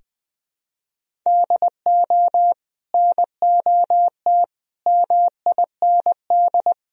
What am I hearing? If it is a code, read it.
Morse code, heard as a keyed tone, about 21 wpm, 700 Hz. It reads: DO NOT MIND
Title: Sentences from Top 1000 Words - Review Part 1 - 20wpm